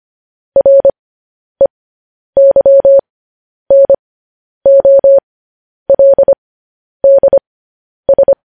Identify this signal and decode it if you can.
Morse code, heard as a keyed tone, about 25 wpm, spaced as for 10 wpm, 550 Hz. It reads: REYNOLDS
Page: News Headlines in Morse Code at 10 WPM